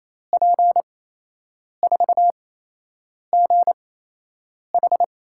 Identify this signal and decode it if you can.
Morse code, heard as a keyed tone, about 28 wpm, spaced as for 8 wpm, 700 Hz. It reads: P4GH
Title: USA Call Signs - 5 Characters - 8x Character Spacing - 28wpm